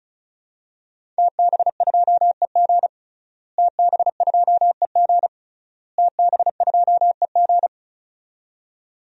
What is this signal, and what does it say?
Morse code, heard as a keyed tone, about 36 wpm, 700 Hz. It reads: TB2EG TB2EG TB2EG
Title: DXCC-100 Call Signs - 5 Characters - Repeated 3x - 35wpm